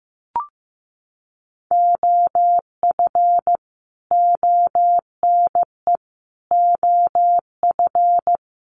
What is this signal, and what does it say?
Morse code, heard as a keyed tone, about 15 wpm, 700 Hz. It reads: OF ONE OF
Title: Sets of 5 Words Encore2 15wpm